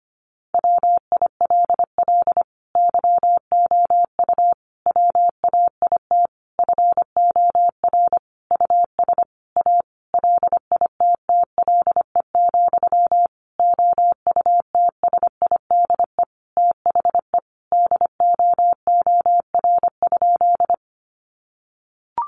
Morse code, heard as a keyed tone, about 25 wpm, 700 Hz. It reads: WILL YOU WAIT FOR US A LITTLE, OUTSIDE THE DOOR?
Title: Sentences from Top 900 Words 25wpm